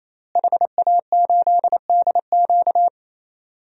Morse code, heard as a keyed tone, about 28 wpm, 700 Hz. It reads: HA8DQ
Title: DXCC-100 Call Signs - 5 Characters - Repeated 3x - 28wpm